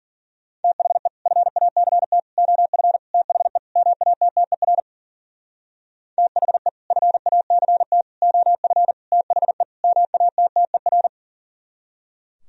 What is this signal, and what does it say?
Morse code, heard as a keyed tone, about 51 wpm, 700 Hz. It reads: THE FACT OF THE MATTER THE FACT OF THE MATTER